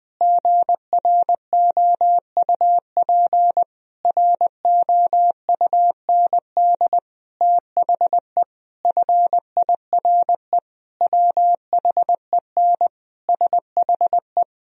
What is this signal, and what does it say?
Morse code, heard as a keyed tone, about 20 wpm, 700 Hz. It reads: GROUP ROUND THE FIRE WHEN SHE
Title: Sentences from Top 1000 Words - Review Part 1 - 20wpm